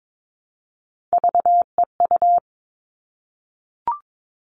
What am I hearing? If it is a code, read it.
Morse code, heard as a keyed tone, about 22 wpm, 700 Hz. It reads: VEU
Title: All Cut Numbers - 3-Digit 22wpm